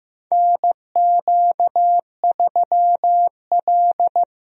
Morse code, heard as a keyed tone, about 15 wpm, 700 Hz. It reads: NQ3L
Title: USA Call Signs - 4 Characters - Repeated 2x - 15wpm